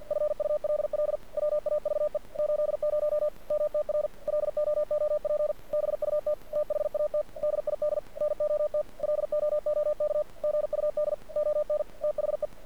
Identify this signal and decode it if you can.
Morse code, heard as a keyed tone, about 53 wpm, 600 Hz, with radio noise. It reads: FULL WAVE 80 MTR LOOP BUT THAT DID NOT LOOK GUD ON THE
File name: archived / CRSnet